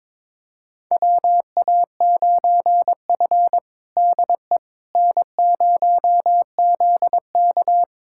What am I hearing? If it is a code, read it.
Morse code, heard as a keyed tone, about 21 wpm, 700 Hz. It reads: WA9F DE N0ZK
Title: USA Callsign DE Callsign - 4 Characters - 22wpm